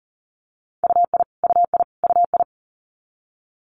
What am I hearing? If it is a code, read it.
Morse code, heard as a keyed tone, about 42 wpm, 700 Hz. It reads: UI UI UI
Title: Multi-letter Phonemes Repeated 3x 40wpm